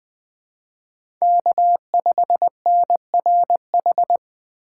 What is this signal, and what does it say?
Morse code, heard as a keyed tone, about 24 wpm, 700 Hz. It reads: K5NRH
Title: Speed Racing - USA 1x3 Call Signs 20wpm